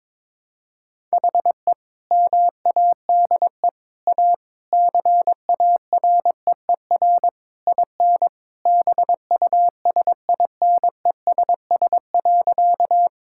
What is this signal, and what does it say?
Morse code, heard as a keyed tone, about 22 wpm, 700 Hz. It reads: HE MADE A CAREER IN BUSINESS.
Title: Sentences from Top 1000 Words - Review Part 6 - 22wpm